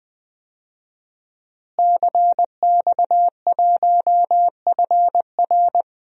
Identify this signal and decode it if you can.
Morse code, heard as a keyed tone, about 20 wpm, 700 Hz. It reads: CX1FR
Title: DXCC-100 Call Signs - 5 Characters - Repeated 2x - 20wpm